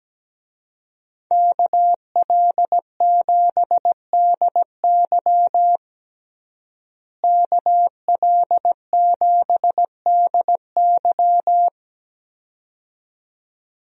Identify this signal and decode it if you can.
Morse code, heard as a keyed tone, about 17 wpm, 700 Hz. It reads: KL7DY KL7DY
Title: DXCC-100 Call Signs - 5 Characters - Repeated 2x - 17wpm